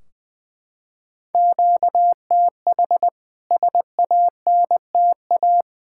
Morse code, heard as a keyed tone, about 20 wpm, 700 Hz. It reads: QTH SANTA
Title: K1USN SST QSO Practice 20wpm